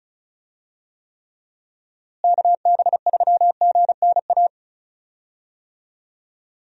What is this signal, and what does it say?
Morse code, heard as a keyed tone, about 36 wpm, 700 Hz. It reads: KB3GNA
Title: USA Call Signs - 6 Characters - Repeated 2x - 35wpm